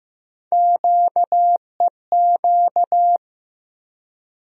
Morse code, heard as a keyed tone, about 15 wpm, 700 Hz. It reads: QEQ
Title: USA Call Sign Postfixes - 3 Letters 15wpm